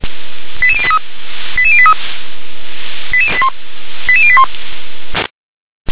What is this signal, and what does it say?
Five-tone selective call (ZVEI), heard as quick runs of five tones, tones of about 80 ms. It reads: 88083 88083 88081
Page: SDR/ZVEI